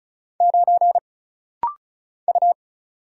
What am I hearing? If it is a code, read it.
Morse code, heard as a keyed tone, about 34 wpm, 700 Hz. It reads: 9 U
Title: Single Letter-Number Alt - Rapid-Fire 35wpm